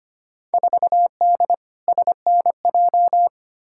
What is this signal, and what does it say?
Morse code, heard as a keyed tone, about 25 wpm, 700 Hz. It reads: 4D SNJ
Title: Field Day Exchanges Encore - 25wpm